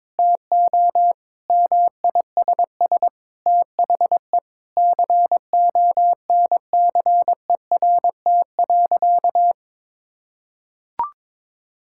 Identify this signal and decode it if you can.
Morse code, heard as a keyed tone, about 22 wpm, 700 Hz. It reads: TO MISS THE CONCERT.
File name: practice / 22wpm / Sentences with 8-Word Phrases - 22wpm